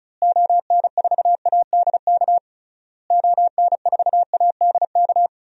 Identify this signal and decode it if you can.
Morse code, heard as a keyed tone, about 36 wpm, 700 Hz. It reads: ON4ADK ON4ADK
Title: DXCC-100 Call Signs - 6 Characters - Repeated 3x - 35wpm